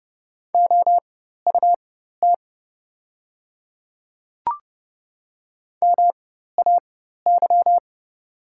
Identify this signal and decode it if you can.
Morse code, heard as a keyed tone, about 29 wpm, spaced as for 14 wpm, 700 Hz. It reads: OUT MAY
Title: 4x Character Spacing - Top 100 Words - 30wpm